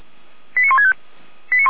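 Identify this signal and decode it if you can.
Five-tone selective call (ZVEI), heard as quick runs of five tones, tones of about 70 ms. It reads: 89165
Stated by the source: ZVEI five-tone selective call 89170 received on an SDR.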